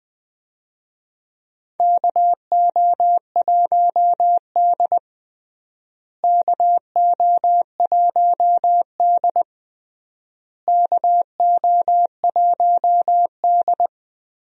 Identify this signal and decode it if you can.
Morse code, heard as a keyed tone, about 20 wpm, 700 Hz. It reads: KO1D KO1D KO1D
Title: USA Call Signs - 4 Characters - Repeated 3x - 20wpm